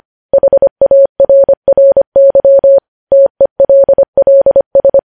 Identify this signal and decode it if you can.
Morse code, heard as a keyed tone, about 25 wpm, 550 Hz. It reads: HARRY TELLS
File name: podcast / News headlines-1x-25x25